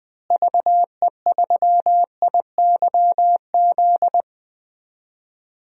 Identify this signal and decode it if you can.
Morse code, heard as a keyed tone, about 20 wpm, 700 Hz. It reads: VE3IYZ
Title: DXCC-100 Call Signs - 6 Characters - Repeated 3x - 20wpm